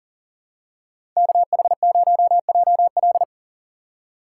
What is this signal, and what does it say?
Morse code, heard as a keyed tone, about 40 wpm, 700 Hz. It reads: KH0JL
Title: USA Call Signs - 5 Characters - 2x Character Spacing - 40wpm